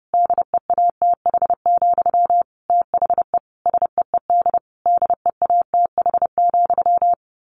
Morse code, heard as a keyed tone, about 30 wpm, 700 Hz. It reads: DEATH, THE SEED DEATH,
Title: A Sound Of Thunder 30wpm